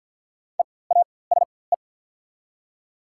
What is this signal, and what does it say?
Morse code, heard as a keyed tone, about 48 wpm, spaced as for 23 wpm, 700 Hz. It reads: EASE
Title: 4x Character Spacing - Top 300 Words - 50wpm